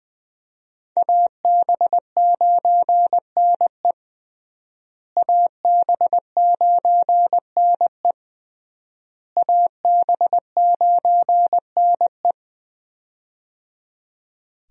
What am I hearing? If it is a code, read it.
Morse code, heard as a keyed tone, about 20 wpm, 700 Hz. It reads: AB9NE AB9NE AB9NE
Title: USA Call Signs - 5 Characters - Repeated 3x - 20wpm